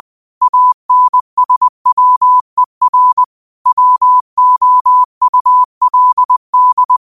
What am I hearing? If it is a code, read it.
Morse code, heard as a keyed tone, about 20 wpm, 1000 Hz. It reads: ANSWER WOULD
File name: qotdfiles / 20wpm